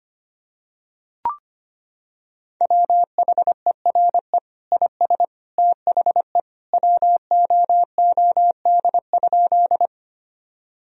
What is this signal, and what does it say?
Morse code, heard as a keyed tone, about 25 wpm, 700 Hz. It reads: WHERE IS THE WOOD?
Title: Sentences from Top 1000 Words - Review Part 2 - 25wpm